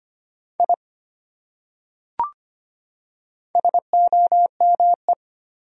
Morse code, heard as a keyed tone, about 25 wpm, 700 Hz. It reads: I SOME
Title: Top 100 Words 25wpm